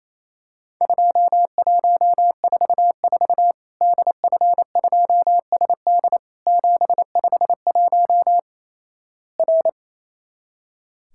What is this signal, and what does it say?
Morse code, heard as a keyed tone, about 28 wpm, 700 Hz. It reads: 2144 DF2SD 751 R
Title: Worked All Europe DX Contest - QTC Exchange - 28wpm